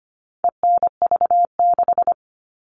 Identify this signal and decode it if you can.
Morse code, heard as a keyed tone, about 25 wpm, 700 Hz. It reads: ENV6
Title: All Cut Numbers - 4-Digit 25wpm